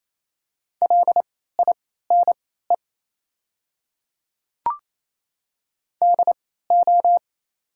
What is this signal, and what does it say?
Morse code, heard as a keyed tone, about 28 wpm, spaced as for 16 wpm, 700 Hz. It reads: LINE DO
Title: 3x Character Spacing - Top 300 Words - 28wpm